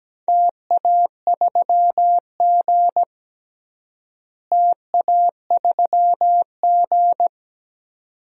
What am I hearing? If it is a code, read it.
Morse code, heard as a keyed tone, about 17 wpm, 700 Hz. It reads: TA3G TA3G
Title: DXCC-100 Call Signs - 4 Characters - Repeated 2x - 17wpm